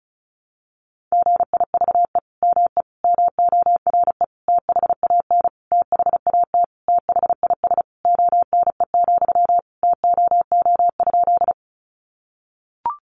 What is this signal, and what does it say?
Morse code, heard as a keyed tone, about 34 wpm, 700 Hz. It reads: GIVE ME MORE THAN THAT THIS ONE, TOO?
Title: Sentences from Top 200 Words 35wpm